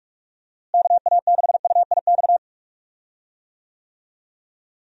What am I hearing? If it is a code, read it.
Morse code, heard as a keyed tone, about 46 wpm, 700 Hz. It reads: KA6UIX
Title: USA Call Signs - 6 Characters - Repeated 2x - 45wpm